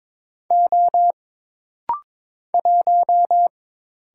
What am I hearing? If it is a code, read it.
Morse code, heard as a keyed tone, about 23 wpm, 700 Hz. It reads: O 1
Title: Single Letter-Number Alt - Rapid-Fire 22wpm